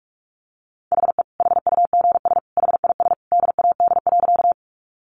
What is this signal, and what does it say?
Morse code, heard as a keyed tone, about 44 wpm, 700 Hz. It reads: HE HUGS HIS DAD.
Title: Easy Phrases 45wpm